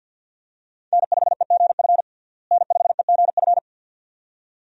Morse code, heard as a keyed tone, about 49 wpm, 700 Hz. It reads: N5EGF N5EGF
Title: USA Call Signs - 5 Characters - Repeated 2x - 50wpm